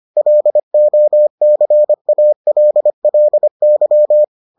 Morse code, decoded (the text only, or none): LOCALLY